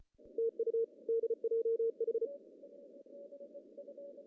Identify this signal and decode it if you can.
Morse code, heard as a keyed tone, about 33 wpm, 440 Hz. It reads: TU DJH
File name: CW / cqww-cw-k5tr